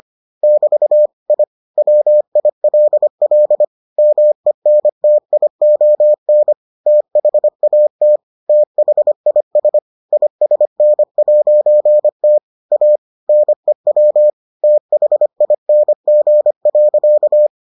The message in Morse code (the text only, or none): <BT> I WILL MENTION THAT THIS ISN'T A NEW THING.